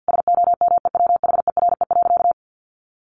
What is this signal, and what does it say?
Morse code, heard as a keyed tone, about 50 wpm, 700 Hz. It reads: SOMEWHERE.
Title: Sentences from Top 1400 Words 50wpm